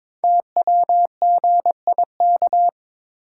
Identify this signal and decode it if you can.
Morse code, heard as a keyed tone, about 22 wpm, 700 Hz. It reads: TWGIK
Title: Code Groups - 5 Letters - 22wpm